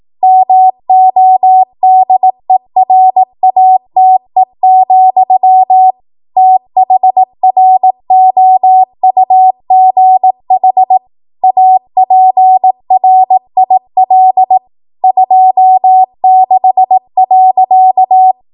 Morse code, heard as a keyed tone, about 18 wpm, 750 Hz. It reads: MODERATE, THROUGH APRIL 26.
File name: Morse / 18WPM